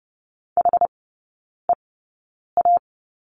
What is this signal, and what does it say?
Morse code, heard as a keyed tone, about 30 wpm, spaced as for 9 wpm, 700 Hz. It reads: HEA
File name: course / 30:10wpm / 087 Lesson 15e - Number 5 - Three Characters 30wpm